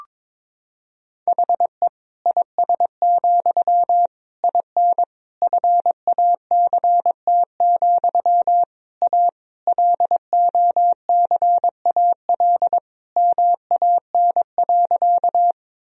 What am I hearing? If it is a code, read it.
Morse code, heard as a keyed tone, about 22 wpm, 700 Hz. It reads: HE IS, IN FACT, A LOCAL MAN.
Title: Sentences from Top 1000 Words - Review Part 3 - 22wpm